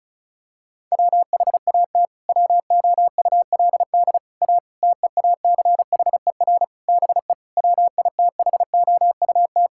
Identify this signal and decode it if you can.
Morse code, heard as a keyed tone, about 36 wpm, 700 Hz. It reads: WHAT WOULD A TEACHER BE WITHOUT
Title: Sentences from Top 1000 Words - Review Part 6 - 35wpm